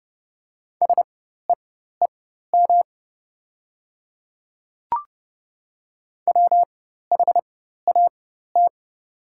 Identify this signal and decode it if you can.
Morse code, heard as a keyed tone, about 30 wpm, spaced as for 14 wpm, 700 Hz. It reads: SEEM WHAT
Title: 4x Character Spacing - Top 300 Words - 30wpm